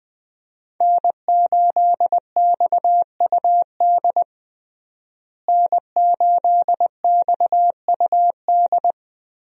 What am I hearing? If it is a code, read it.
Morse code, heard as a keyed tone, about 21 wpm, 700 Hz. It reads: N8XUD N8XUD
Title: USA Call Signs - 5 Characters - Repeated 2x - 20wpm